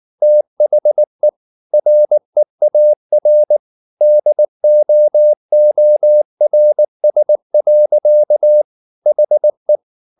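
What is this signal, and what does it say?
Morse code, heard as a keyed tone, about 19 wpm, 600 Hz. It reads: THE REAR DOORS. HE
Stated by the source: Morse code at 19 wpm from CW med Gnister, reading Ken Follett's novel Never in English.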